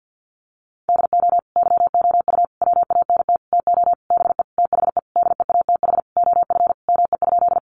Morse code, heard as a keyed tone, about 50 wpm, 700 Hz. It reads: DO YOU WANT TO BE THE DEATH OF ME?
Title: Sentences from Top 1100 Words 50wpm